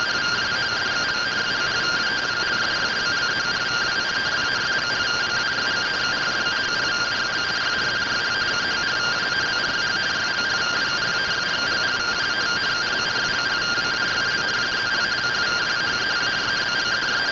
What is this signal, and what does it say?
Two-tone teleprinter signal, about 170 Hz shift.